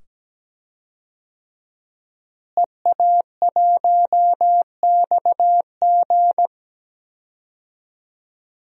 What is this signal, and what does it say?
Morse code, heard as a keyed tone, about 17 wpm, 700 Hz. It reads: EA1XG
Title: DXCC-100 Call Signs - 5 Characters - Repeated 2x - 17wpm